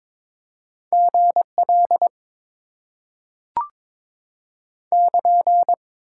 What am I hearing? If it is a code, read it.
Morse code, heard as a keyed tone, about 22 wpm, 700 Hz. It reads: GL <KN>
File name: Top 90 QSO Elements - 8x Character Spacing - 22wpm